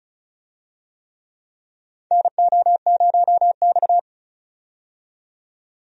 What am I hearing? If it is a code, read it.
Morse code, heard as a keyed tone, about 36 wpm, 700 Hz. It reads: NO0X
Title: USA Call Signs - 4 Characters - 35wpm